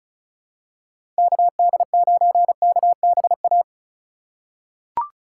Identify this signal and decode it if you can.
Morse code, heard as a keyed tone, about 36 wpm, 700 Hz. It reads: KD9KBA
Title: USA Call Signs - 6 Characters - 4x Character Spacing - 35wpm